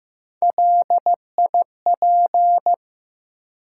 Morse code, heard as a keyed tone, about 15 wpm, 700 Hz. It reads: LIP
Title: Three Letter Words - 15wpm